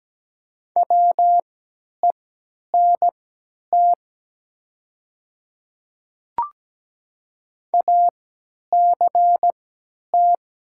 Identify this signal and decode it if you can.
Morse code, heard as a keyed tone, about 17 wpm, spaced as for 10 wpm, 700 Hz. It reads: WENT ACT
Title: 3x Character Spacing - Top 200 Words - 17wpm